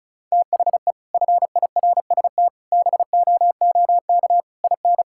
Morse code, heard as a keyed tone, about 34 wpm, 700 Hz. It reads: THE FIRST BOOK IN
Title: Sentences from Top 1000 Words - Review Part 5 - 35wpm